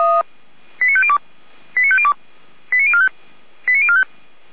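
Five-tone selective call (ZVEI), heard as quick runs of five tones, tones of about 70 ms. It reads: 89502 89502 89045 89045
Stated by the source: Five-tone ZVEI selective call 89078 received by SDR.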